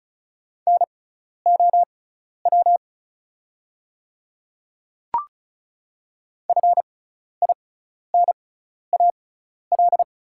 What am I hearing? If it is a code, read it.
Morse code, heard as a keyed tone, about 36 wpm, spaced as for 12 wpm, 700 Hz. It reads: NOW FINAL
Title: 6x Character Spacing - Top 500 Words - 35wpm